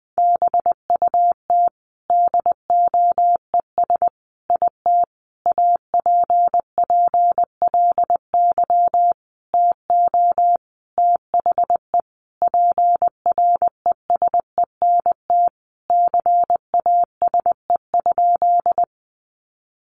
Morse code, heard as a keyed tone, about 19 wpm, 700 Hz. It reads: BUT DOES IT APPLY TO THE PRESENT CASE?
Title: Sentences from Top 1200 Words 20wpm